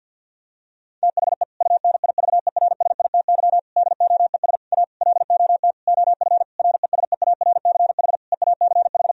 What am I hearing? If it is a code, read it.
Morse code, heard as a keyed tone, about 50 wpm, 700 Hz. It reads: THE UNIVERSITY DOES A LOT OF RESEARCH EACH
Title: Sentences from Top 1000 Words - Review Part 4 - 50wpm